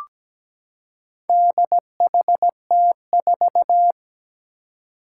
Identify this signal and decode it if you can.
Morse code, heard as a keyed tone, about 17 wpm, 700 Hz. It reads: DHT4
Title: Four Challenging Letter-Number Combinations 17wpm